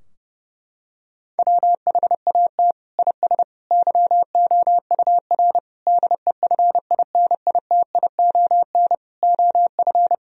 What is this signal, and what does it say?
Morse code, heard as a keyed tone, about 30 wpm, 700 Hz. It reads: WHAT IS YOUR DEFINITION OF
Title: Sentences from Top 1000 Words - Review Part 4 - 30wpm